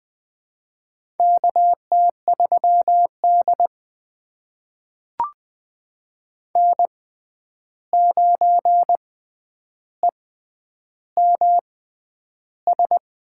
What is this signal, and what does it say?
Morse code, heard as a keyed tone, about 20 wpm, spaced as for 10 wpm, 700 Hz. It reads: KT3D N9EMS
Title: USA Call Signs - 6x Character Spacing - 20wpm